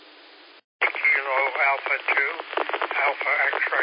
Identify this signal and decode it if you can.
Emergency Alert System (SAME) tones, heard as harsh buzzing bursts, with speech. It is heard from 0.8 to 3.6 s.